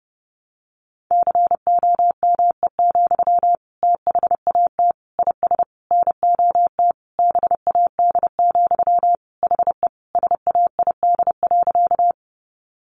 Morse code, heard as a keyed tone, about 30 wpm, 700 Hz. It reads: COME, THAT IS NOT BAD, HE SAID.
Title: Sentences from Top 700 Words 30wpm